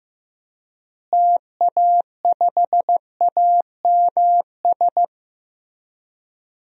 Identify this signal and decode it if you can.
Morse code, heard as a keyed tone, about 15 wpm, 700 Hz. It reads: TA5AMS
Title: DXCC-100 Call Signs - 6 Characters - Repeated 3x - 15wpm